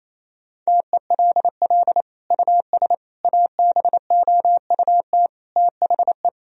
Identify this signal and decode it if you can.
Morse code, heard as a keyed tone, about 28 wpm, 700 Hz. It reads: TELL US ABOUT THE
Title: Sentences from Top 1000 Words - Review Part 3 - 28wpm